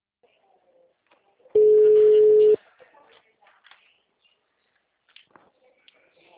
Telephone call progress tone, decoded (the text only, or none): ringback tone